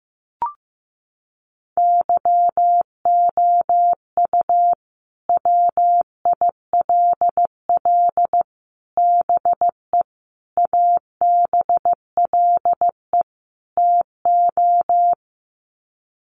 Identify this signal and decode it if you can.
Morse code, heard as a keyed tone, about 15 wpm, 700 Hz. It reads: YOU WILL BE ABLE TO
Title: Sets of 5 Words Encore2 15wpm